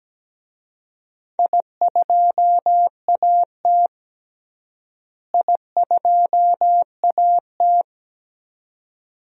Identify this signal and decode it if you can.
Morse code, heard as a keyed tone, about 17 wpm, 700 Hz. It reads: I2AT I2AT
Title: DXCC-100 Call Signs - 4 Characters - Repeated 2x - 17wpm